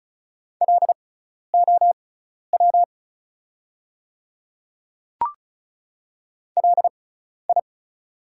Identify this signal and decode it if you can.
Morse code, heard as a keyed tone, about 36 wpm, spaced as for 12 wpm, 700 Hz. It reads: LOW LI
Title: 6x Character Spacing - Top 200 Words - 35wpm